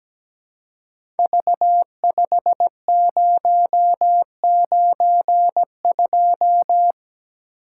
Morse code, heard as a keyed tone, about 17 wpm, 700 Hz. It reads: V5092
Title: Code Groups - 5 Letter:Numbers - 17wpm